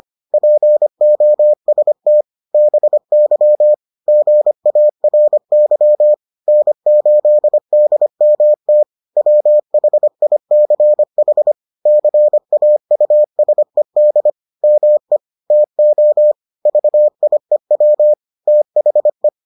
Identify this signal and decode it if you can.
Morse code, heard as a keyed tone, about 25 wpm, 600 Hz. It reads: POST BY GARY N8DMT WHICH CAUSED ME TO VIEW THE